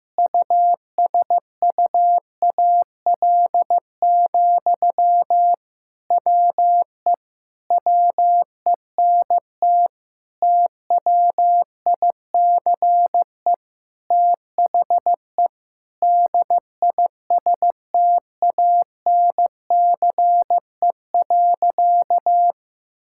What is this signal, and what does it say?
Morse code, heard as a keyed tone, about 15 wpm, 700 Hz. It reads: USUAL, WE WENT TWICE THE DISTANCE.
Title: Sentences from Top 1600 Words 15wpm